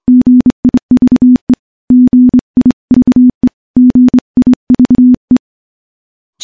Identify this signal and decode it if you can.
Morse code, heard as a keyed tone, about 25 wpm, 250 Hz. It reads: GIVE GIVE GIVE
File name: morse / morse-training-25WPM